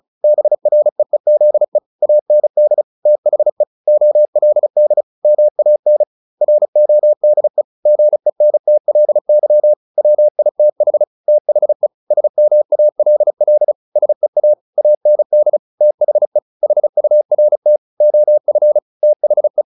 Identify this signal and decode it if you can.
Morse code, heard as a keyed tone, about 34 wpm, 600 Hz. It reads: BREEZE AND THE OLD MAN RODE GENTLY WITH THE SMALL SEA AND THE HURT OF THE
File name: the old man and the sea / the old man and the sea 1 35wpm